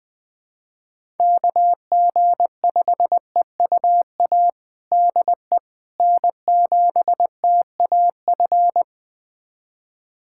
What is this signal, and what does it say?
Morse code, heard as a keyed tone, about 20 wpm, 700 Hz. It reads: KG5EUA DE N7TAF
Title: USA Callsign DE Callsign 20wpm